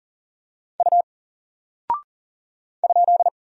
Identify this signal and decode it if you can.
Morse code, heard as a keyed tone, about 39 wpm, 700 Hz. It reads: U ?
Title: Single Character (Letter:Number:Punctuation:Prosign) 40wpm